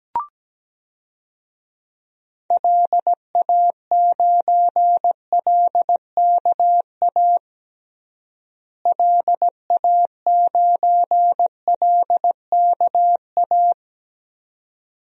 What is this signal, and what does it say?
Morse code, heard as a keyed tone, about 17 wpm, 700 Hz. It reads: LA9LKA LA9LKA
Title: DXCC-100 Call Signs - 6 Characters - Repeated 2x - 17wpm